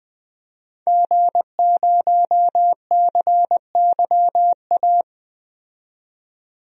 Morse code, heard as a keyed tone, about 20 wpm, 700 Hz. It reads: G0CYA
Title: DXCC-100 Call Signs - 5 Characters - 20wpm